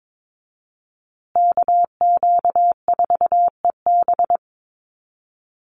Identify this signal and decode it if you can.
Morse code, heard as a keyed tone, about 22 wpm, 700 Hz. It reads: KQ4EB
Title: USA Call Signs Repeated 3x 22wpm